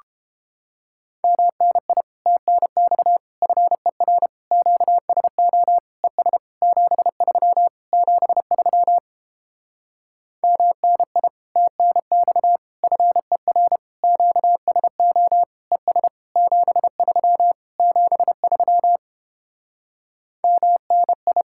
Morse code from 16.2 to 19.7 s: 73 73

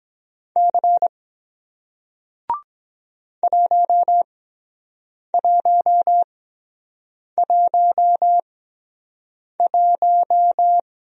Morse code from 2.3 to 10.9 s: 1111

